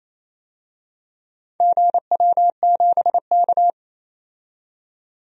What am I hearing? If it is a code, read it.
Morse code, heard as a keyed tone, about 28 wpm, 700 Hz. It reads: GW7K
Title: DXCC-100 Call Signs - 4 Characters - 28wpm